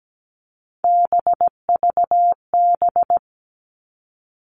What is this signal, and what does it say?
Morse code, heard as a keyed tone, about 17 wpm, 700 Hz. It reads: BVB
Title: All Cut Numbers - 3-Digit 17wpm